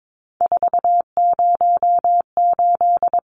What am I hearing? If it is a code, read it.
Morse code, heard as a keyed tone, about 22 wpm, 700 Hz. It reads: 408
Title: Numbers - 3-Digit 22wpm